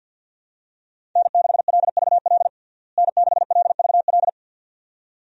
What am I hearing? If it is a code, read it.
Morse code, heard as a keyed tone, about 51 wpm, 700 Hz. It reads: N6LVL N6LVL
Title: USA Call Signs - 5 Characters - Repeated 2x - 50wpm